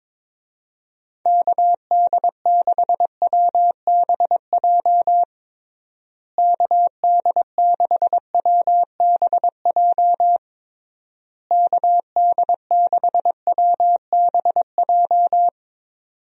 Morse code, heard as a keyed tone, about 22 wpm, 700 Hz. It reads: KD6WBJ KD6WBJ KD6WBJ
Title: USA Call Signs - 6 Characters - Repeated 3x - 22wpm